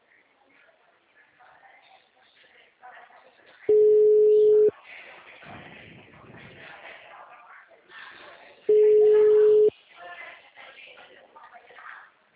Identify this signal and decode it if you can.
Ringback tone (Europe, 425 Hz), with speech.
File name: out- -2006- - - .47712